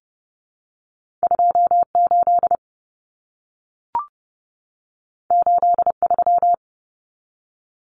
Morse code, heard as a keyed tone, about 32 wpm, 700 Hz. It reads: 28 83
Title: Numbers - 2-Digit 30wpm